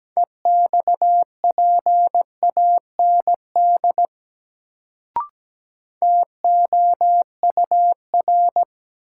Morse code, heard as a keyed tone, about 17 wpm, 700 Hz. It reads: EXPAND TOUR
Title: Sentences from Top 1400 Words - New Word Drill Alt - 17wpm